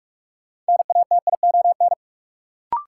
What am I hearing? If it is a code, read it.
Morse code, heard as a keyed tone, about 42 wpm, 700 Hz. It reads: NATION
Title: Sets of 3 Words - New Word Drill - 45wpm